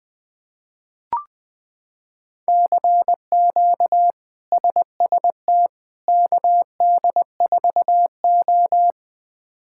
Morse code, heard as a keyed tone, about 20 wpm, 700 Hz. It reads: CQ SST KD4O
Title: K1USN SST QSO Practice - 20wpm